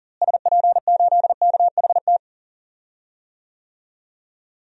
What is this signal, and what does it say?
Morse code, heard as a keyed tone, about 40 wpm, 700 Hz. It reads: SP8KHT
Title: DXCC-100 Call Signs - 6 Characters - Repeated 2x - 40wpm